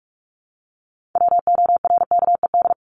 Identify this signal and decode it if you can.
Morse code, heard as a keyed tone, about 44 wpm, 700 Hz. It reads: WORKED
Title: ED Suffix - Top 100 Words with Base Words 45wpm